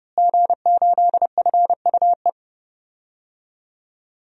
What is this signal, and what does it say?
Morse code, heard as a keyed tone, about 30 wpm, 700 Hz. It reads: G8FUE
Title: DXCC-100 Call Signs - 5 Characters - Repeated 3x - 30wpm